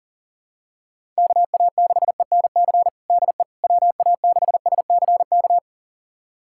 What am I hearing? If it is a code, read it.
Morse code, heard as a keyed tone, about 40 wpm, 700 Hz. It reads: KA6ENC DE WA6SCK
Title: USA Callsign DE Callsign - 6 Characters - 40wpm